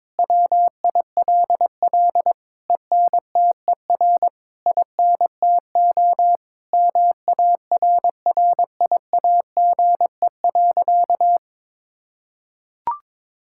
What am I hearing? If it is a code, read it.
Morse code, heard as a keyed tone, about 22 wpm, 700 Hz. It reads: WILL ENTER INTO MARRIAGE.
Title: Sentences from Top 1000 Words - Review Part 6 - 22wpm